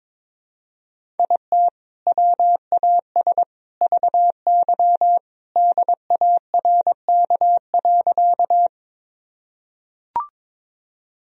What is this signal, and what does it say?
Morse code, heard as a keyed tone, about 22 wpm, 700 Hz. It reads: IT WAS VY DARK.
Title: Sentences from Top 500 Words and Abbreviations 22wpm